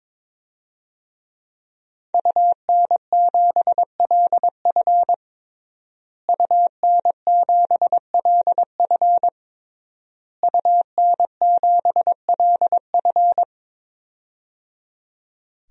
Morse code, heard as a keyed tone, about 22 wpm, 700 Hz. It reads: UN7LF UN7LF UN7LF
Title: DXCC-100 Call Signs - 5 Characters - Repeated 3x - 22wpm